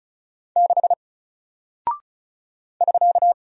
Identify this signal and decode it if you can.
Morse code, heard as a keyed tone, about 35 wpm, 700 Hz. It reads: 6 <SK>
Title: Single Character (Letter:Number:Punctuation:Prosign) Alt 35wpm